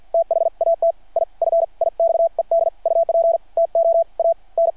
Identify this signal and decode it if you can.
Morse code, heard as a keyed tone, about 52 wpm, spaced as for 40 wpm, 600 Hz. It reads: THAT I FIXED UP TO AN
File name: archived / CRSnet